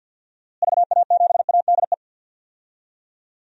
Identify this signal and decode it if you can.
Morse code, heard as a keyed tone, about 51 wpm, 700 Hz. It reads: VA7ADE
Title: DXCC-100 Call Signs - 6 Characters - Repeated 2x - 50wpm